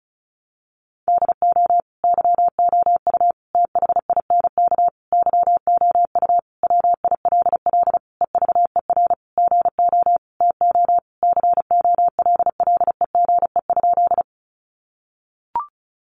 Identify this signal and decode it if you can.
Morse code, heard as a keyed tone, about 34 wpm, 700 Hz. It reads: DO YOU THINK YOU WILL EVER GO TO COLLEGE?
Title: Sentences from Top 1300 Words 35wpm